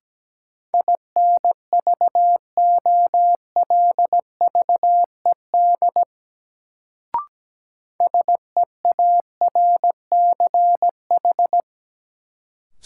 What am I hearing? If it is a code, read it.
Morse code, heard as a keyed tone, about 17 wpm, 700 Hz. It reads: INVOLVED SEARCH
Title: Sentences from Top 800 Words - New Word Drill - 17wpm